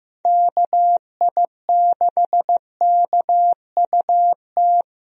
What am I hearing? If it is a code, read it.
Morse code, heard as a keyed tone, about 15 wpm, 700 Hz. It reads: KI6KUT
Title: USA Call Signs - 6 Characters - 15wpm